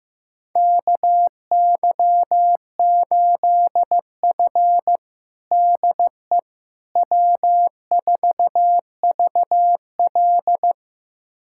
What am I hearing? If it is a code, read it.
Morse code, heard as a keyed tone, about 15 wpm, 700 Hz. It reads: KY8F DE W4VL
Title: USA Callsign DE Callsign - 4 Characters - 15wpm